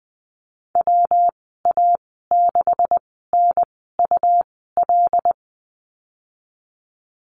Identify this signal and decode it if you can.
Morse code, heard as a keyed tone, about 20 wpm, spaced as for 15 wpm, 700 Hz. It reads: WA6NUL
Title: 207 Lesson 33g - Letter j - Call Signs 20wpm